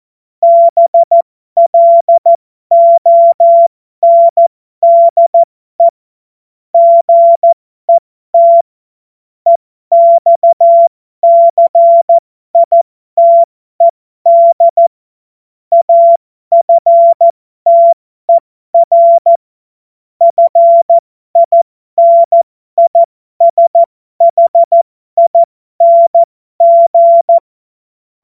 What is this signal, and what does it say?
Morse code, heard as a keyed tone, about 14 wpm, spaced as for 12 wpm, 680 Hz. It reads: BLONDE GET EXCITED AFTER FINISHING